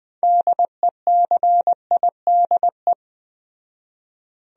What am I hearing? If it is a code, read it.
Morse code, heard as a keyed tone, about 20 wpm, 700 Hz. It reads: DECIDE
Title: Speed-Racing Top 500 Words 17wpm